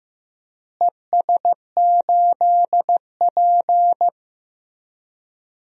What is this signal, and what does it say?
Morse code, heard as a keyed tone, about 15 wpm, 700 Hz. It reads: ES8P
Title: DXCC-100 Call Signs - 4 Characters - Repeated 3x - 15wpm